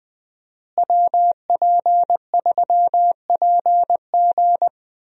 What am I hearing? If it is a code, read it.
Morse code, heard as a keyed tone, about 20 wpm, 700 Hz. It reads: WP3PG
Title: USA Call Signs - 5 Characters - Repeated 3x - 20wpm